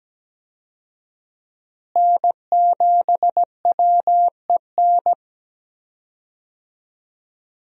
Morse code, heard as a keyed tone, about 17 wpm, 700 Hz. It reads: N7WEN
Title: USA Call Signs - 5 Characters - Repeated 3x - 17wpm